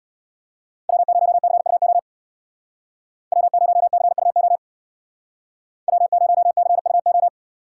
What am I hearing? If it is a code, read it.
Morse code, heard as a keyed tone, about 62 wpm, 700 Hz. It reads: W0CUO W0CUO W0CUO
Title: Speed Racing - USA 1x3 Call Signs 50wpm